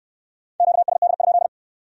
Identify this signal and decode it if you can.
Morse code, heard as a keyed tone, about 66 wpm, 700 Hz. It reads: QSD?